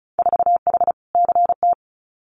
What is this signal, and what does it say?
Morse code, heard as a keyed tone, about 36 wpm, 700 Hz. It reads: 4H CT
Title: Winter Field Day Exchanges Encore 35wpm